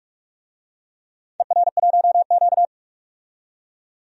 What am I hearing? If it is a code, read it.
Morse code, heard as a keyed tone, about 42 wpm, 700 Hz. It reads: ER1Q
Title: DXCC-100 Call Signs - 4 Characters - 45wpm